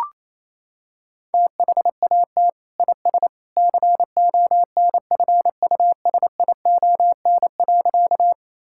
Morse code, heard as a keyed tone, about 28 wpm, 700 Hz. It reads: THAT IS CONFUSION.
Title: Sentences from Top 1600 Words 28wpm